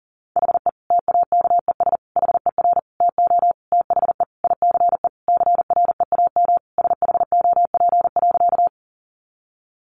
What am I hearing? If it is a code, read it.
Morse code, heard as a keyed tone, about 40 wpm, 700 Hz. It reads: HE TAKES HER TO THE ICE CREAM SHOP.
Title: Easy Phrases 40wpm